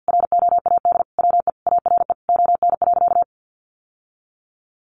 Morse code, heard as a keyed tone, about 50 wpm, 700 Hz. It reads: ROAD WE ARE ON.
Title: Sentences from Top 1100 Words 50wpm